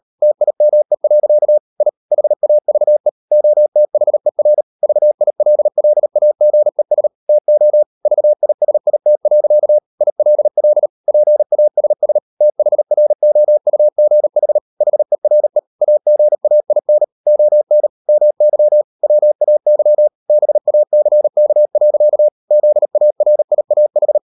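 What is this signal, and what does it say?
Morse code, heard as a keyed tone, about 38 wpm, 600 Hz. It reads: TIME. I HAVE OTHER VILLAGES TO VISIT. ILL PASS THROUGH HERE AGAIN ON MY WAY BACK. ZARIAH
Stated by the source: Morse code practice at 38 WPM, sending Ken Follett's novel Never in English, from CW med Gnister.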